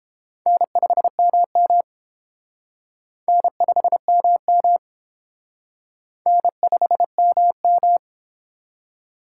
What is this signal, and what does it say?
Morse code, heard as a keyed tone, about 28 wpm, 700 Hz. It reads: N5MM N5MM N5MM